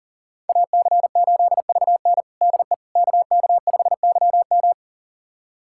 Morse code, heard as a keyed tone, about 40 wpm, 700 Hz. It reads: AC8VN DE KK5YM
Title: USA Callsign DE Callsign - 5 Characters - 40wpm